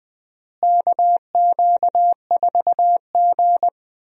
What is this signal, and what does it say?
Morse code, heard as a keyed tone, about 20 wpm, 700 Hz. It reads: KQ4G
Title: USA Call Signs - 4 Characters - Repeated 2x - 20wpm